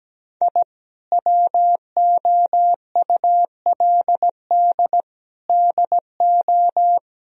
Morse code, heard as a keyed tone, about 17 wpm, 700 Hz. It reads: I WOULD DO
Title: Sentences from Top 1000 Words - Review Part 2 - 17wpm